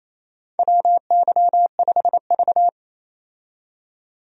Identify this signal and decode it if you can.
Morse code, heard as a keyed tone, about 28 wpm, 700 Hz. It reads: WY5V
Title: USA Call Signs - 4 Characters - 28wpm